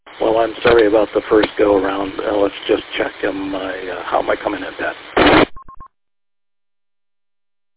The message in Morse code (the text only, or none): none